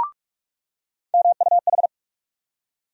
Morse code, heard as a keyed tone, about 42 wpm, 700 Hz. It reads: MUH
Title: USA Call Sign Postfixes - 3 Letters 45wpm